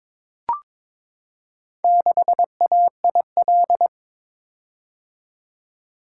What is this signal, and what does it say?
Morse code, heard as a keyed tone, about 22 wpm, 700 Hz. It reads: 6AIL
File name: practice / 22wpm / Four Letter-Number Combinations - 22wpm